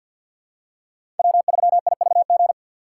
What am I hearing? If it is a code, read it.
Morse code, heard as a keyed tone, about 51 wpm, 700 Hz. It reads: W3IVG
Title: USA Call Signs - 5 Characters - 4x Character Spacing - 50wpm